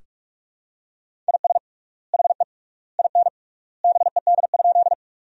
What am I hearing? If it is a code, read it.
Morse code, heard as a keyed tone, about 41 wpm, 700 Hz. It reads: IS HE IN BED?
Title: 3x Word-Spacing - Sentences from Top 500 Words - 45wpm